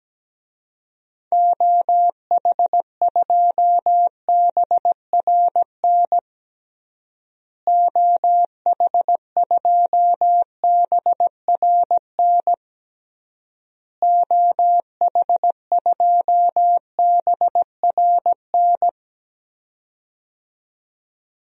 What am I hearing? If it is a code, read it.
Morse code, heard as a keyed tone, about 17 wpm, 700 Hz. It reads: OH2BRN OH2BRN OH2BRN
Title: DXCC-100 Call Signs - 6 Characters - Repeated 3x - 17wpm